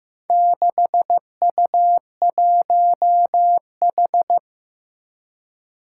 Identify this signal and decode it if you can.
Morse code, heard as a keyed tone, about 15 wpm, 700 Hz. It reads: 6U1H
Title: Four Challenging Letter-Number Combinations 15wpm